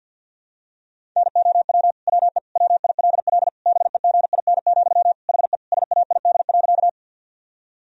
Morse code, heard as a keyed tone, about 50 wpm, 700 Hz. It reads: NOW WE WILL BEGIN, HE SAID.
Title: Sentences from Top 1000 Words - Review Part 1 - 50wpm